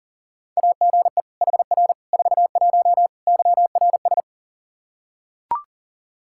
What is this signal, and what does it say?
Morse code, heard as a keyed tone, about 40 wpm, 700 Hz. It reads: AGE HR 41 YRS
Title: Practice QSO - Exchange 3 - More Info - 40wpm